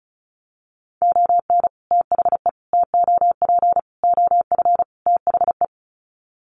Morse code, heard as a keyed tone, about 34 wpm, 700 Hz. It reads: ON THE TOP OF THE
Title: Sets of 5 Words Encore 35wpm